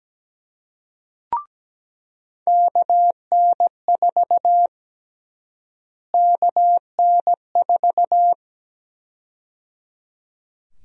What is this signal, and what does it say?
Morse code, heard as a keyed tone, about 17 wpm, 700 Hz. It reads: KN4 KN4
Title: USA Call Sign Prefixes - Repeated 2x - 17wpm